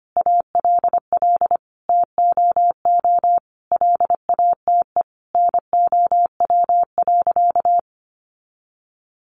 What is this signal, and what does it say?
Morse code, heard as a keyed tone, about 25 wpm, 700 Hz. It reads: ALL TOO LATE NOW.
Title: Sentences from Top 300 Words 25wpm